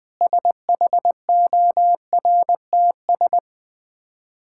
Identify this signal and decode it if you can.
Morse code, heard as a keyed tone, about 20 wpm, 690 Hz. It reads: SHORTS
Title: Difficult Words with Hints - 20wpm